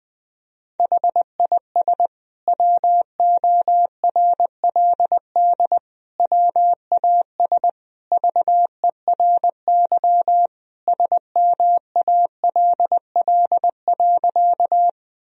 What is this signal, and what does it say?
Morse code, heard as a keyed tone, about 21 wpm, 700 Hz. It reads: HIS WORLD WAS VERY SMALL.
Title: Sentences from Top 1000 Words - Review Part 1 - 20wpm